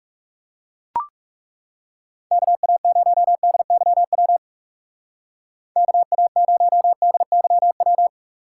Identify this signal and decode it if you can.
Morse code, heard as a keyed tone, about 37 wpm, 700 Hz. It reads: KA0DYW KA0DYW
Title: Speed Racing - USA Call Signs 30wpm